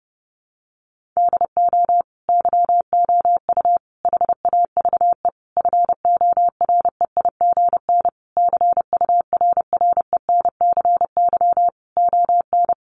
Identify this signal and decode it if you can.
Morse code, heard as a keyed tone, about 30 wpm, 700 Hz. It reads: DO YOU HAVE FOREIGN CURRENCY ON
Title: Sentences from Top 1500 Words 30wpm